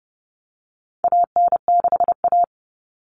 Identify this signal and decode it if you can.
Morse code, heard as a keyed tone, about 30 wpm, 700 Hz. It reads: AN6A
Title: All Cut Numbers - 4-Digit 30wpm